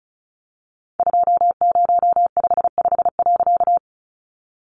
Morse code, heard as a keyed tone, about 34 wpm, 700 Hz. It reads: 2055.
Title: A Sound Of Thunder 35wpm